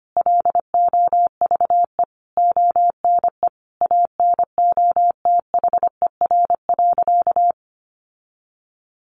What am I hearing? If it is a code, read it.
Morse code, heard as a keyed tone, about 25 wpm, 700 Hz. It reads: LOVE ONE ANOTHER.
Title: Sentences from Top 600 Words 25wpm